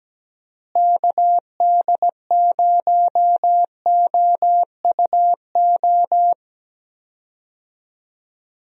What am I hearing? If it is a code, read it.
Morse code, heard as a keyed tone, about 17 wpm, 700 Hz. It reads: KD0OUO
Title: USA Call Signs - 6 Characters - Repeated 3x - 17wpm